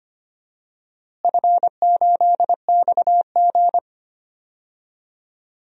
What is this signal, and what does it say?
Morse code, heard as a keyed tone, about 25 wpm, 700 Hz. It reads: F8XG
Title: DXCC-100 Call Signs - 4 Characters - Repeated 2x - 25wpm